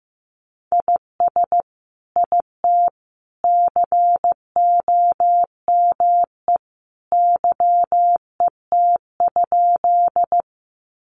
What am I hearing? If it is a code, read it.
Morse code, heard as a keyed tone, about 15 wpm, 700 Hz. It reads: IS IT COME YET?
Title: Sentences from Top 500 Words 15wpm